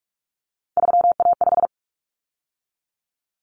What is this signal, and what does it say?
Morse code, heard as a keyed tone, about 44 wpm, 700 Hz. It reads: 3A5
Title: Common Cut Numbers - 3-Digit 45wpm